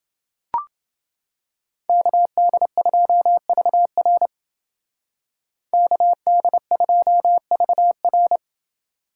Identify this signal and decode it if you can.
Morse code, heard as a keyed tone, about 28 wpm, 700 Hz. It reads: KD2VR KD2VR